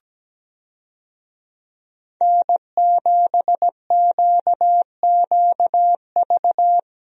Morse code, heard as a keyed tone, about 17 wpm, 700 Hz. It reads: N7QQV
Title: USA Call Signs - 5 Characters - 17wpm